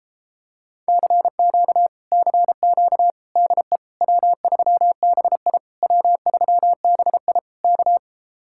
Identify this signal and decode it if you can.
Morse code, heard as a keyed tone, about 29 wpm, 700 Hz. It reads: CQ CQ DE W3BI W3BI K